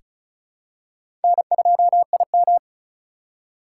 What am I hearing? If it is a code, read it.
Morse code, heard as a keyed tone, about 34 wpm, 700 Hz. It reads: N2IM
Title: USA Call Signs - 4 Characters - 4x Character Spacing - 35wpm